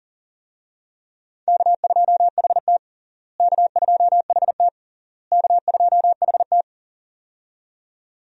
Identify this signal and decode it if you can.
Morse code, heard as a keyed tone, about 40 wpm, 700 Hz. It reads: K2HT K2HT K2HT
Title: USA Call Signs - 4 Characters - Repeated 3x - 40wpm